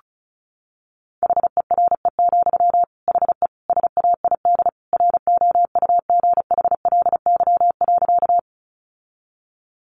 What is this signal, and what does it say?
Morse code, heard as a keyed tone, about 34 wpm, 700 Hz. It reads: HERE, HE SAID ROUGHLY.
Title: Sentences from Top 1500 Words 35wpm